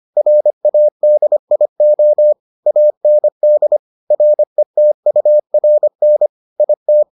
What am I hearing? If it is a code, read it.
Morse code, heard as a keyed tone, about 25 wpm, 600 Hz. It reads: RADIO AND RETURN IT